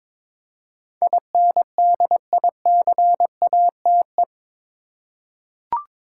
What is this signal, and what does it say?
Morse code, heard as a keyed tone, about 21 wpm, 700 Hz. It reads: INDICATE
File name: practice / 22wpm / Eight Letter Words - 22wpm